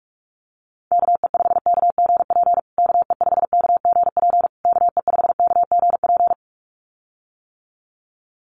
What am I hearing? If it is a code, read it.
Morse code, heard as a keyed tone, about 44 wpm, 700 Hz. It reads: KE5KGP KE5KGP KE5KGP
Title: USA Call Signs Repeated 3x 45wpm